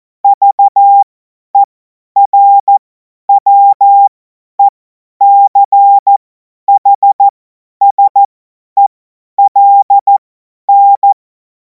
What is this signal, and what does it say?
Morse code, heard as a keyed tone, about 14 wpm, spaced as for 10 wpm, 800 Hz. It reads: VERWECHSELN
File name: Morsekurs / Witze / Witz 16. Juni 2026 50 F70